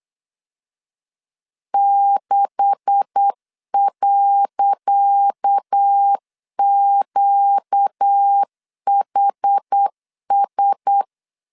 Morse code, heard as a keyed tone, about 8 wpm, 800 Hz. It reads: 6.QHS
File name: LEZIONI / 10 SIGNS / 10C DOT 8wpm 3649